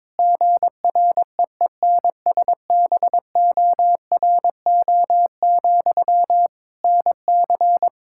Morse code, heard as a keyed tone, about 22 wpm, 700 Hz. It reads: GREENSBORO, NC
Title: Practice QSO - Exchange 1 - Beginning - 22wpm